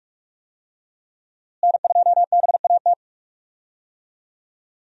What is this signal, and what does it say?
Morse code, heard as a keyed tone, about 46 wpm, 700 Hz. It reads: N2BAT
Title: USA Call Signs - 5 Characters - Repeated 2x - 45wpm